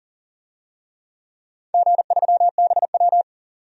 Morse code, heard as a keyed tone, about 40 wpm, 700 Hz. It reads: G3BW
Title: DXCC-100 Call Signs - 4 Characters - 40wpm